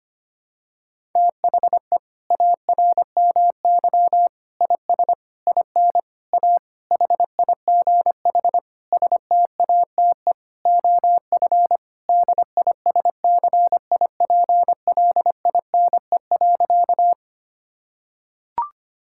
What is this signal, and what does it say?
Morse code, heard as a keyed tone, about 25 wpm, 700 Hz. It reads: THE ARMY IS IN A HIGH STATE OF DISCIPLINE.
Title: Sentences from Top 1000 Words - Review Part 6 - 25wpm